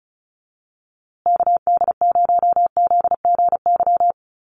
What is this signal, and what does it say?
Morse code, heard as a keyed tone, about 34 wpm, 700 Hz. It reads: KD0ZGY
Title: USA Call Signs Repeated 3x 35wpm